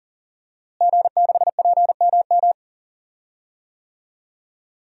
Morse code, heard as a keyed tone, about 40 wpm, 700 Hz. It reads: G6PMM
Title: DXCC-100 Call Signs - 5 Characters - Repeated 2x - 40wpm